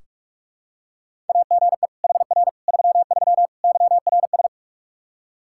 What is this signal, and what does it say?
Morse code, heard as a keyed tone, about 44 wpm, 700 Hz. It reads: AGE HR 33 YRS
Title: Practice QSO - Exchange 3 - More Info - 45wpm